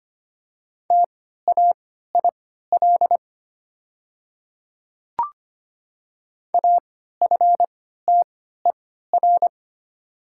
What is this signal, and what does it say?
Morse code, heard as a keyed tone, about 25 wpm, spaced as for 14 wpm, 700 Hz. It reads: TAIL AFTER
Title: 3x Character Spacing - Top 500 Words - 25wpm